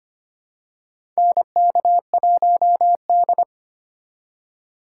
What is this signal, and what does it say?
Morse code, heard as a keyed tone, about 25 wpm, 700 Hz. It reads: NK1D
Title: USA Call Signs - 4 Characters - 3x Character Spacing - 25wpm